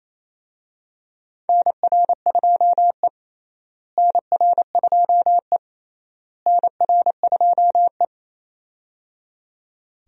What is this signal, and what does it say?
Morse code, heard as a keyed tone, about 28 wpm, 700 Hz. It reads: NR2E NR2E NR2E
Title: USA Call Signs - 4 Characters - Repeated 3x - 28wpm